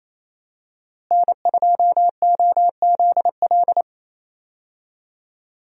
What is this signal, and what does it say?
Morse code, heard as a keyed tone, about 28 wpm, 700 Hz. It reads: N2OZL
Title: USA Call Signs - 5 Characters - Repeated 3x - 28wpm